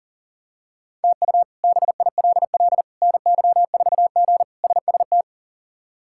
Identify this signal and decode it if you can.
Morse code, heard as a keyed tone, about 40 wpm, 700 Hz. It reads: TU BILL NY4G SST
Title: K1USN SST QSO Practice - 40wpm